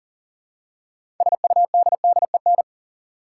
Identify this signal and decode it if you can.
Morse code, heard as a keyed tone, about 40 wpm, 700 Hz. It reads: SUDDEN
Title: Six Letter Words - 40wpm